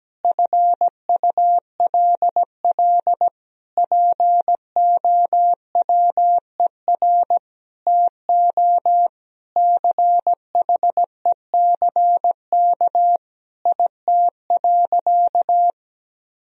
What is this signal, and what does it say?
Morse code, heard as a keyed tone, about 17 wpm, 700 Hz. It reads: FULL POWER TO CHECK IT.
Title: Sentences from Top 1000 Words - Review Part 3 - 17wpm